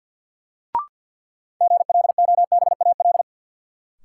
Morse code, heard as a keyed tone, about 50 wpm, 700 Hz. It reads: GLOBAL
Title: Sentences from Top 1200 Words - New Word Drill - 50wpm